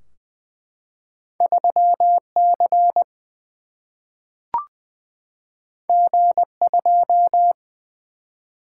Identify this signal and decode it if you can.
Morse code, heard as a keyed tone, about 20 wpm, 700 Hz. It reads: 3 C G 2
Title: Single-Letter Single-Number 20wpm